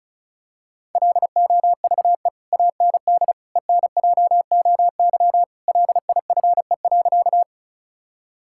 Morse code, heard as a keyed tone, about 36 wpm, 700 Hz. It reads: LOVE AND ENJOY LIFE.
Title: Sentences from Top 1000 Words - Review Part 5 - 35wpm